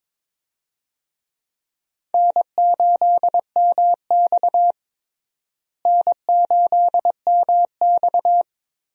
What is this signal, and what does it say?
Morse code, heard as a keyed tone, about 22 wpm, 700 Hz. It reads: N8MX N8MX
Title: USA Call Signs - 4 Characters - Repeated 2x - 22wpm